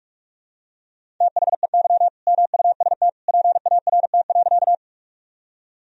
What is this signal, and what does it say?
Morse code, heard as a keyed tone, about 46 wpm, 700 Hz. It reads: THEY MUST PART.
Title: Sentences from Top 1000 Words - Review Part 1 - 45wpm